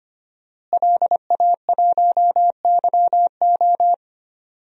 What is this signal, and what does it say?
Morse code, heard as a keyed tone, about 25 wpm, 700 Hz. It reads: LA1YO
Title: DXCC-100 Call Signs - 5 Characters - Repeated 3x - 25wpm